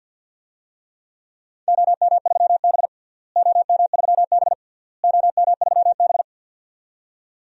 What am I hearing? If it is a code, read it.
Morse code, heard as a keyed tone, about 53 wpm, 700 Hz. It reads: OM3B OM3B OM3B
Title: DXCC-100 Call Signs - 4 Characters - Repeated 3x - 50wpm